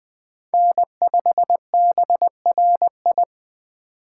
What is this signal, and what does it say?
Morse code, heard as a keyed tone, about 21 wpm, 700 Hz. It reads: N5BRI